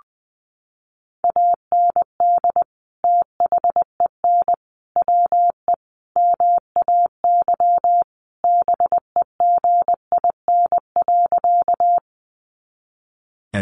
Morse code, heard as a keyed tone, about 20 wpm, 700 Hz. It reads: AND THEN WE MAY BEGIN.
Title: Sentences from Top 300 Words 20wpm